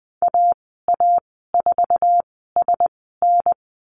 Morse code, heard as a keyed tone, about 20 wpm, spaced as for 14 wpm, 700 Hz. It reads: AA4SN
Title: 062 Lesson 11g - Letter h - Call Signs 20wpm